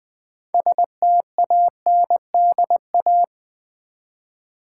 Morse code, heard as a keyed tone, about 20 wpm, 700 Hz. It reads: STANDA
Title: Build A Word - 3 Letters At A Time - 8-Letter Words - 20wpm